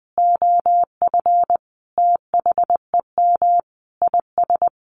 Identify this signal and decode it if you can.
Morse code, heard as a keyed tone, about 20 wpm, 700 Hz. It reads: OF THEM IS
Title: Sentences from Top 1100 Words 20wpm